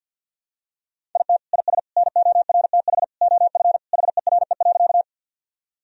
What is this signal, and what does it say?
Morse code, heard as a keyed tone, about 49 wpm, 700 Hz. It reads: IT IS NORTH OF HERE.
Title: Sentences from Top 1000 Words - Review Part 1 - 50wpm